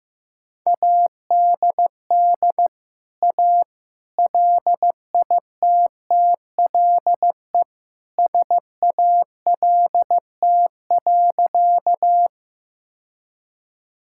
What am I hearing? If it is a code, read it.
Morse code, heard as a keyed tone, about 16 wpm, 700 Hz. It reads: ADD A LITTLE SALT.
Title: Sentences from Top 1000 Words - Review Part 5 - 15wpm